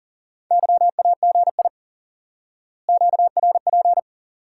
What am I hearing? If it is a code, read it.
Morse code, heard as a keyed tone, about 40 wpm, 700 Hz. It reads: YAGI QRP
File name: IQE Course 3 - Lesson 89 - YAGI - Two Elements - 40wpm